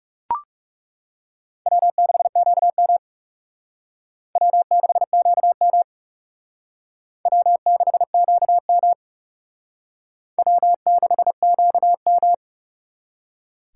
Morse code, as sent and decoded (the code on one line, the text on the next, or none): .-- -.... --.- -- / .-- -.... --.- -- / .-- -.... --.- -- / .-- -.... --.- --
W6QM W6QM W6QM W6QM